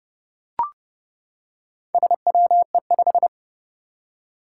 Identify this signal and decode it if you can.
Morse code, heard as a keyed tone, about 32 wpm, 700 Hz. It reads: SWE5
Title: Four Challenging Letter-Number Combinations 30wpm